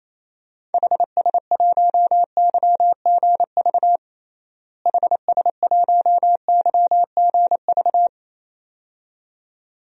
Morse code, heard as a keyed tone, about 28 wpm, 700 Hz. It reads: HS1YGV HS1YGV
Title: DXCC-100 Call Signs - 6 Characters - Repeated 2x - 28wpm